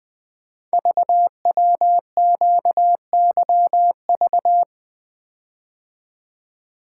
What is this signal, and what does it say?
Morse code, heard as a keyed tone, about 20 wpm, 700 Hz. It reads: VWQYV
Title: Code Groups - 5 Letters - 20wpm